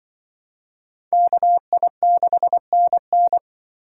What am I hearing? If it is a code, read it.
Morse code, heard as a keyed tone, about 23 wpm, 700 Hz. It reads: KI6NN
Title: Speed Racing - USA 2x2 Call Signs 20wpm